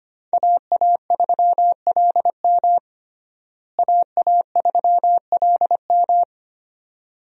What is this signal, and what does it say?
Morse code, heard as a keyed tone, about 25 wpm, 700 Hz. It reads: AA3LM AA3LM
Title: USA Call Signs - 5 Characters - Repeated 2x - 25wpm